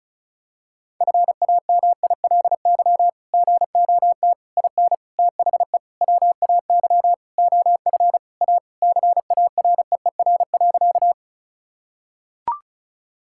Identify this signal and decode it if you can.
Morse code, heard as a keyed tone, about 34 wpm, 700 Hz. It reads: FAMILY GOT IN THE WAY OF A CAREER.
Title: Sentences from Top 1000 Words - Review Part 6 - 35wpm